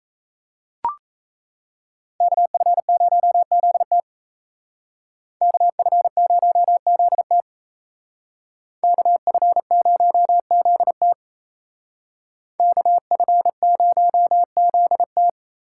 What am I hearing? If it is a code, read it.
Morse code, heard as a keyed tone, about 36 wpm, 700 Hz. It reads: KF0ZT KF0ZT KF0ZT KF0ZT